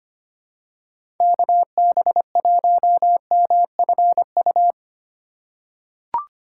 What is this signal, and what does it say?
Morse code, heard as a keyed tone, about 25 wpm, 700 Hz. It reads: KB1MFU
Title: USA Call Signs - 3x Character Spacing - 25wpm